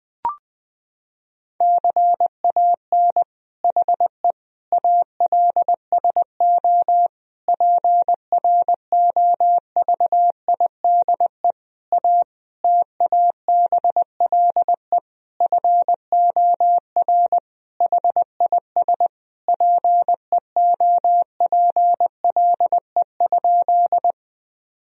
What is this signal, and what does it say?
Morse code, heard as a keyed tone, about 20 wpm, 700 Hz. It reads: CAN HE ALSO PROVIDE A TABLE FOR HIS PEOPLE?
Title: Sentences from Top 1000 Words - Review Part 3 - 20wpm